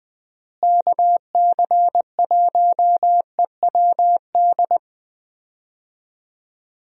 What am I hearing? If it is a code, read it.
Morse code, heard as a keyed tone, about 20 wpm, 700 Hz. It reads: KC1EWD
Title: USA Call Signs - 6 Characters - Repeated 2x - 20wpm